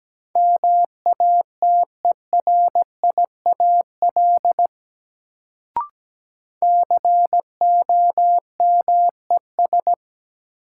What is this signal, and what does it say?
Morse code, heard as a keyed tone, about 17 wpm, 700 Hz. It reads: MATERIAL COMES
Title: Sentences from Top 1000 Words - Review Part 5 - New Word Drill - 17wpm